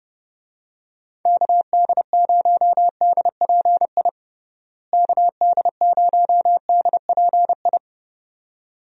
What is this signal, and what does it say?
Morse code, heard as a keyed tone, about 31 wpm, 700 Hz. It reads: KD0DPI KD0DPI
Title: USA Call Signs - 6 Characters - Repeated 2x - 30wpm